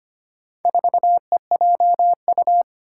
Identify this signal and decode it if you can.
Morse code, heard as a keyed tone, about 25 wpm, 700 Hz. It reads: 4EJU
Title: Four Challenging Letter-Number Combinations 25wpm